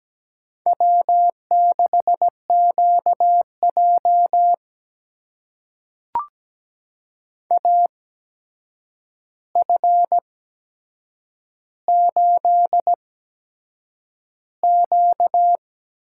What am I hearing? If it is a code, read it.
Morse code, heard as a keyed tone, about 17 wpm, spaced as for 7 wpm, 700 Hz. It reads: W6QJ AF8Q
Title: USA Call Signs - 4 Characters - 8x Character Spacing - 17wpm